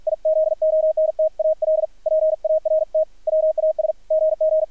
Morse code, heard as a keyed tone, about 50 wpm, 630 Hz, with radio noise. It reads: IZONTAL PART WAS GG
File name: archived / CRSnet